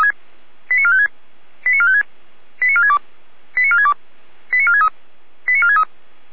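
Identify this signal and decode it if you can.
Five-tone selective call (ZVEI), heard as quick runs of five tones, tones of about 70 ms. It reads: 89456 89456 89462 89462 89463 89463
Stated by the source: ZVEI five-tone selective call received on an SDR.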